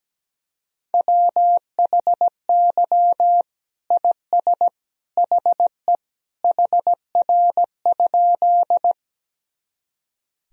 Morse code, heard as a keyed tone, about 17 wpm, 700 Hz. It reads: WHY IS HE HR?
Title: Sentences from Top 200 Words and Abbreviations 17wpm